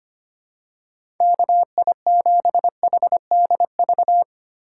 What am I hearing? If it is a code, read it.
Morse code, heard as a keyed tone, about 25 wpm, 700 Hz. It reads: KI7HDV
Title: USA Call Signs - 6 Characters - 2x Character Spacing - 25wpm